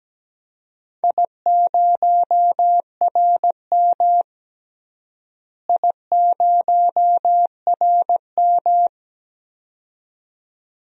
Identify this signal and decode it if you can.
Morse code, heard as a keyed tone, about 17 wpm, 700 Hz. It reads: I0RM I0RM
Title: DXCC-100 Call Signs - 4 Characters - Repeated 2x - 17wpm